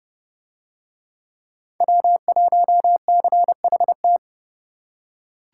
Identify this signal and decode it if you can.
Morse code, heard as a keyed tone, about 30 wpm, 700 Hz. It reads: W1CHT